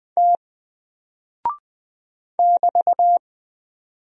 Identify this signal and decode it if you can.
Morse code, heard as a keyed tone, about 20 wpm, 700 Hz. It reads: T <BT>
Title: Single Character (Letter:Number:Punctuation:Prosign) 20wpm